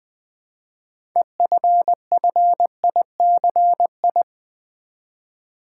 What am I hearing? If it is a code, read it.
Morse code, heard as a keyed tone, about 20 wpm, 700 Hz. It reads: EFFICI
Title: Build A Word - 3 Letters At A Time - 9-Letter Words - 20wpm